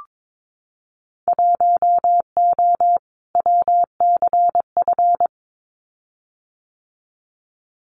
Morse code, heard as a keyed tone, about 22 wpm, 700 Hz. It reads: 1O WCF
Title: Winter Field Day Exchanges 22wpm